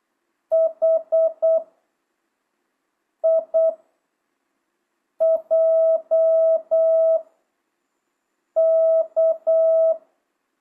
Morse code, heard as a keyed tone, about 8 wpm, spaced as for 4 wpm, 660 Hz. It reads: HIJK
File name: jh8cbh / 2025cw